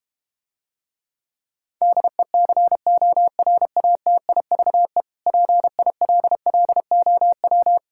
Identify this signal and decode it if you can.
Morse code, heard as a keyed tone, about 32 wpm, 700 Hz. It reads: DECORATIVE PILLOW